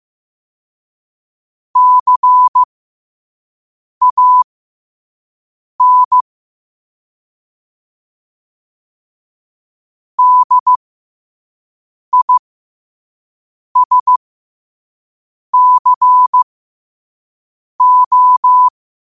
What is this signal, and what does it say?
Morse code, heard as a keyed tone, about 15 wpm, spaced as for 5 wpm, 1000 Hz. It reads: CAN DISCO
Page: New quotes every day in morse code at 5 Words per minute.